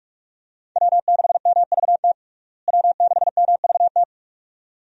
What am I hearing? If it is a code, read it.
Morse code, heard as a keyed tone, about 46 wpm, 700 Hz. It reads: W6MVT W6MVT
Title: USA Call Signs - 5 Characters - Repeated 2x - 45wpm